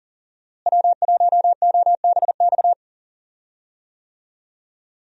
Morse code, heard as a keyed tone, about 37 wpm, 700 Hz. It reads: W1OBX
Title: USA Call Signs - 5 Characters - Repeated 3x - 40wpm